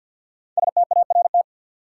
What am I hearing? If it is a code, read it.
Morse code, heard as a keyed tone, about 49 wpm, 700 Hz. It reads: START
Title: Sentences from Top 300 Words - New Word Drill - 50wpm